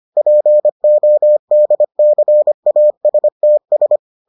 Morse code, heard as a keyed tone, about 25 wpm, 600 Hz. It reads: PODCASTS